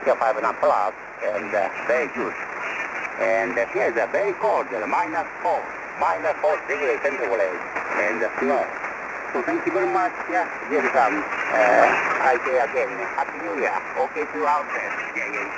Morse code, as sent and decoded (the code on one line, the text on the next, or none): none